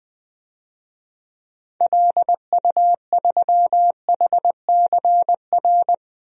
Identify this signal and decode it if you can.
Morse code, heard as a keyed tone, about 20 wpm, 700 Hz. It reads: LU3HCR
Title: DXCC-100 Call Signs - 6 Characters - 20wpm